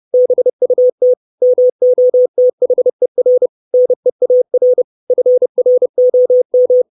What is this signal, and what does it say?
Morse code, heard as a keyed tone, about 30 wpm, 500 Hz. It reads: BUT MOTHER NEAR FROM
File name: 632 Words 30WPM